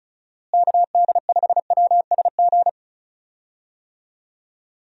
Morse code, heard as a keyed tone, about 34 wpm, 700 Hz. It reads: KD5WSG
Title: USA Call Signs - 6 Characters - Repeated 3x - 35wpm